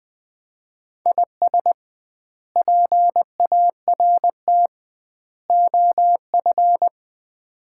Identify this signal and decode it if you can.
Morse code, heard as a keyed tone, about 20 wpm, 700 Hz. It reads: IS PART OF
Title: 2x Word-Spacing - Sets of 4 Words - 20wpm